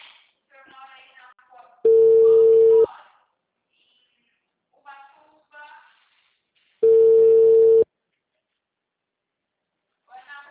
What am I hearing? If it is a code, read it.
Ringback tone (China, 450 Hz), with speech.